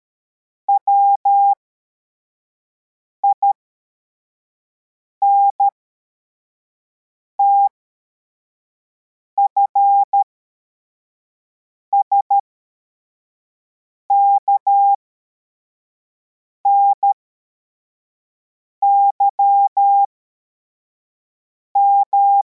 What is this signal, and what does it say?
Morse code, heard as a keyed tone, about 13 wpm, spaced as for 4 wpm, 800 Hz. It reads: WINTF SKNYM
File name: formation / cw / Text63